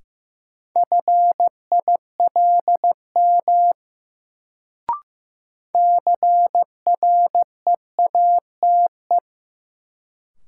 Morse code, heard as a keyed tone, about 15 wpm, 700 Hz. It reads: FILM CREATE
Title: Sentences from Top 1000 Words - Review Part 3 - New Word Drill - 15wpm